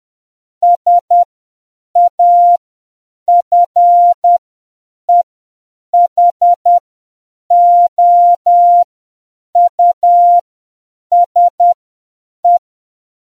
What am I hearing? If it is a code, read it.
Morse code, heard as a keyed tone, about 10 wpm, spaced as for 7 wpm, 700 Hz. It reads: SAFEHOUSE